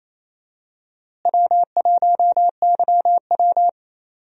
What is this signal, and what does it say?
Morse code, heard as a keyed tone, about 28 wpm, 700 Hz. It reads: W1YW
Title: USA Call Signs - 4 Characters - Repeated 2x - 28wpm